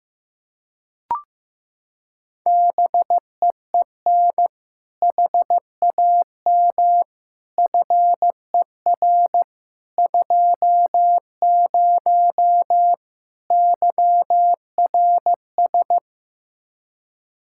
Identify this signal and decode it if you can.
Morse code, heard as a keyed tone, about 15 wpm, 700 Hz. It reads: BEEN HAM FER 20 YRS
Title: K1USN SST QSO Practice 15wpm